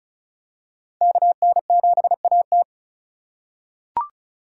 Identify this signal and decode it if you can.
Morse code, heard as a keyed tone, about 36 wpm, 700 Hz. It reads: KN7AT
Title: USA Call Signs - 5 Characters - 4x Character Spacing - 35wpm